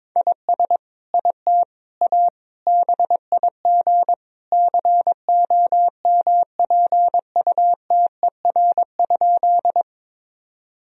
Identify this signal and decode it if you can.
Morse code, heard as a keyed tone, about 21 wpm, 700 Hz. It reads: IS IT A BIG COMPUTER?
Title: Sentences from Top 1000 Words - Review Part 3 - 22wpm